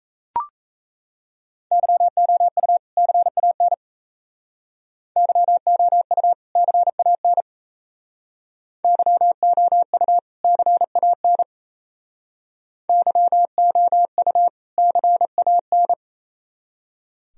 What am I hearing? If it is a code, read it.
Morse code, heard as a keyed tone, about 36 wpm, 690 Hz. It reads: YOU CAN YOU CAN YOU CAN YOU CAN